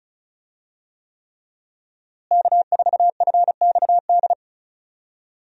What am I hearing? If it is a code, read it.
Morse code, heard as a keyed tone, about 36 wpm, 700 Hz. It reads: K4FXD
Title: USA Call Signs - 5 Characters - Repeated 3x - 35wpm